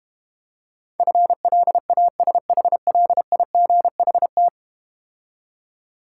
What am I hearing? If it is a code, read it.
Morse code, heard as a keyed tone, about 32 wpm, 700 Hz. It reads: FLASHLIGHT